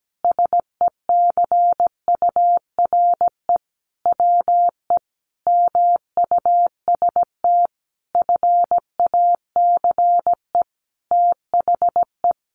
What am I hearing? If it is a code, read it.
Morse code, heard as a keyed tone, about 17 wpm, 700 Hz. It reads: SECURE WE MUST FACE THE
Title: Sentences from Top 1200 Words 17wpm